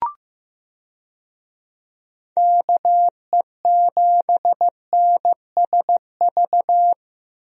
Morse code, heard as a keyed tone, about 15 wpm, 700 Hz. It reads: KE7NSV
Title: USA Call Signs - 6 Characters - Repeated 2x - 15wpm